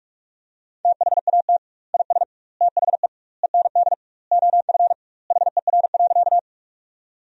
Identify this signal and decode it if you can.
Morse code, heard as a keyed tone, about 41 wpm, 700 Hz. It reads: THAT IS THE END OF HER.
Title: 2x Word-Spacing - Sentences from Top 200 Words - 45wpm